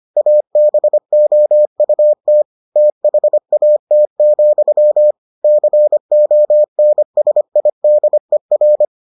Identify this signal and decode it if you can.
Morse code, heard as a keyed tone, about 25 wpm, 600 Hz. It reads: ABOUT THAT, CONSIDER